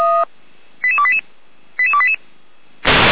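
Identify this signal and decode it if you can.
Five-tone selective call (ZVEI), heard as quick runs of five tones, tones of about 70 ms. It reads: 88288 88288
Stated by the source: ZVEI five-tone selective call received on SDR.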